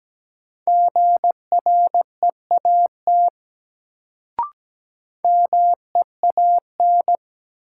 Morse code, heard as a keyed tone, about 17 wpm, 700 Hz. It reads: GREAT MEAN
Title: Sentences from Top 200 Words - New Word Drill - 17wpm